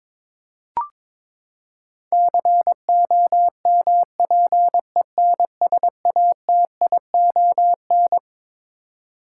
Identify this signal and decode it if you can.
Morse code, heard as a keyed tone, about 22 wpm, 700 Hz. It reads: COMPENSATION
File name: Twelve Letter Words - 22wpm